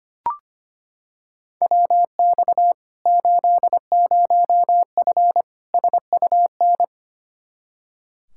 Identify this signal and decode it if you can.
Morse code, heard as a keyed tone, about 25 wpm, 700 Hz. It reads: WX 80F SUN
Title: Practice QSO - Exchange 2 - Info - 25wpm